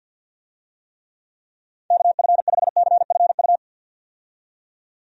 Morse code, heard as a keyed tone, about 50 wpm, 700 Hz. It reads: KF5CFU
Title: USA Call Signs - 6 Characters - 50wpm